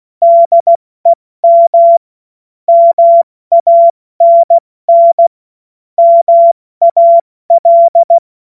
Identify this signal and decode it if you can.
Morse code, heard as a keyed tone, about 16 wpm, 680 Hz. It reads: DEM MANN MAL
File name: Morsekurs / Witze / Witz 12 März 2024 70 F80